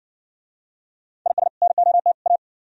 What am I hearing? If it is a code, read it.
Morse code, heard as a keyed tone, about 58 wpm, 700 Hz. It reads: IS NOT A